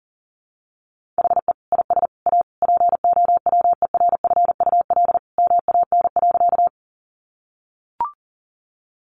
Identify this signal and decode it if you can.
Morse code, heard as a keyed tone, about 40 wpm, 700 Hz. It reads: HE IS A POWERFUL MAN.
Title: Sentences from Top 1300 Words 40wpm